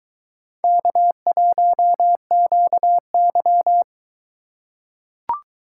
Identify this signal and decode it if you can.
Morse code, heard as a keyed tone, about 23 wpm, 700 Hz. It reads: K1QY